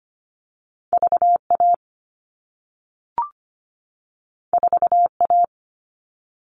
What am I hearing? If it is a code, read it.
Morse code, heard as a keyed tone, about 25 wpm, 700 Hz. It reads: VA 4A
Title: All Cut Numbers - 2-Digit 25wpm